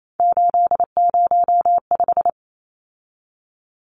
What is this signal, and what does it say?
Morse code, heard as a keyed tone, about 28 wpm, 700 Hz. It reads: 805
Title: Numbers - 3-Digit 28wpm